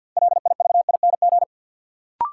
Morse code, heard as a keyed tone, about 52 wpm, 700 Hz. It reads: LIVING
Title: Sets of 4 Words - New Word Drill - 50wpm